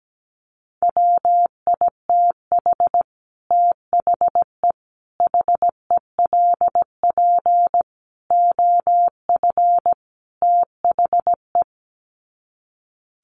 Morse code, heard as a keyed tone, about 17 wpm, 700 Hz. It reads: WITH THE HELP OF THE
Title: Sets of 5 Words Encore 17wpm